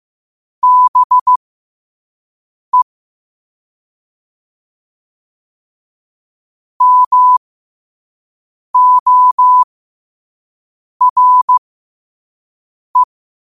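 Morse code, heard as a keyed tone, about 15 wpm, spaced as for 5 wpm, 1000 Hz. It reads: BE MORE
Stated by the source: Morse at 5 wpm.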